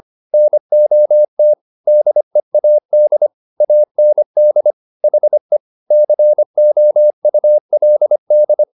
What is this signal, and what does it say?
Morse code, heard as a keyed tone, about 25 wpm, 600 Hz. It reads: NOT DEAD AND HE COULD